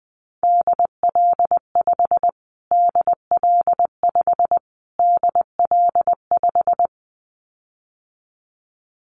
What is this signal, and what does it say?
Morse code, heard as a keyed tone, about 20 wpm, 700 Hz. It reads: DL5 DL5 DL5
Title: Callsign Prefixes for Top 25 DXCC - Repeated 3x - 20wpm